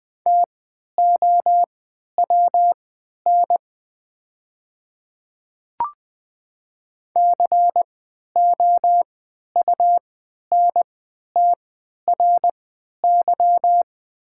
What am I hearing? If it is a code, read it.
Morse code, heard as a keyed tone, about 21 wpm, spaced as for 12 wpm, 690 Hz. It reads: TOWN COUNTRY
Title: 3x Character Spacing - Top 500 Words - 20wpm